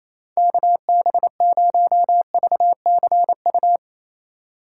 Morse code, heard as a keyed tone, about 36 wpm, 700 Hz. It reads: KB0VCU